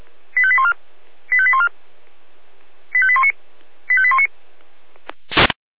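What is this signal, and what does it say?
Five-tone selective call (ZVEI), heard as quick runs of five tones, tones of about 70 ms. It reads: 85814 85814 85819 85819
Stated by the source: ZVEI five-tone call received on an SDR.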